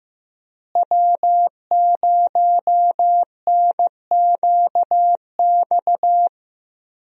Morse code, heard as a keyed tone, about 15 wpm, 700 Hz. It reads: W0NQX
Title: USA Call Signs - 5 Characters - Repeated 2x - 15wpm